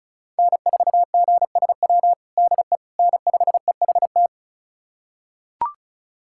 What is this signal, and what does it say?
Morse code, heard as a keyed tone, about 36 wpm, 700 Hz. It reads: N4GSW DE N5EHT
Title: USA Callsign DE Callsign - 5 Characters - 35wpm